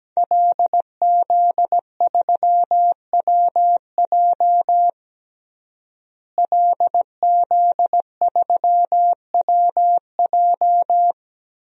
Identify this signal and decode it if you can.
Morse code, heard as a keyed tone, about 17 wpm, 700 Hz. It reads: LZ3WJ LZ3WJ
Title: DXCC-100 Call Signs - 5 Characters - Repeated 2x - 17wpm